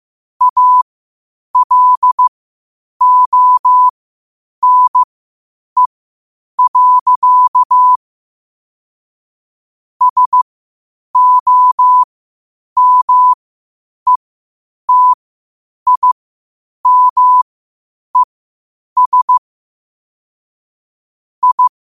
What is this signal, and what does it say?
Morse code, heard as a keyed tone, about 15 wpm, spaced as for 8 wpm, 1000 Hz. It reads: ALONE. SOMETIMES I
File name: qotdfiles / 8wpm